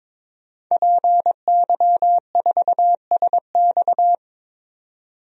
Morse code, heard as a keyed tone, about 22 wpm, 700 Hz. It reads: PY4SX
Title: DXCC-100 Call Signs - 5 Characters - 22wpm